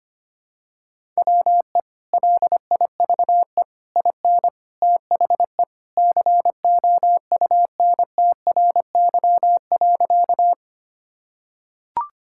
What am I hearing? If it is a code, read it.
Morse code, heard as a keyed tone, about 25 wpm, 700 Hz. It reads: WE LIVE IN THE COUNTRY.
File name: practice / 25wpm / Sentences with 3-Word Phrases - 25wpm